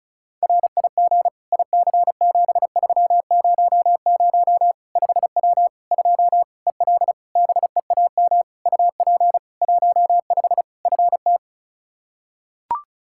Morse code, heard as a keyed tone, about 36 wpm, 700 Hz. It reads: RIG IC7300 5W 2 EL BEAM UP 15 FT
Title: K1USN SST QSO Practice 35wpm